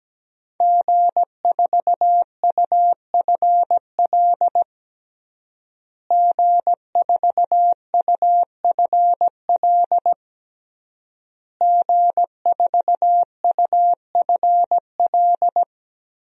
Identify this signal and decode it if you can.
Morse code, heard as a keyed tone, about 17 wpm, 700 Hz. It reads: G4UFL G4UFL G4UFL
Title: DXCC-100 Call Signs - 5 Characters - Repeated 3x - 17wpm